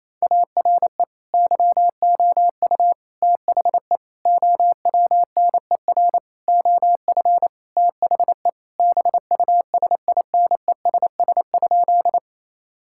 Morse code, heard as a keyed tone, about 28 wpm, 700 Hz. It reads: ARE YOU THE OWNER OF THE BUSINESS?
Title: Sentences from Top 1600 Words 28wpm